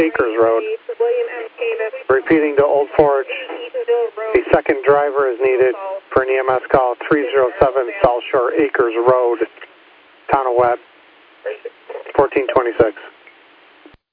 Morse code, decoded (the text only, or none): none